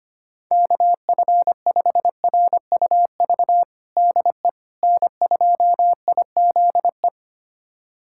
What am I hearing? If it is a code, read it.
Morse code, heard as a keyed tone, about 25 wpm, 700 Hz. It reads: KF5RUV DE N2IZE
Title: USA Callsign DE Callsign 25wpm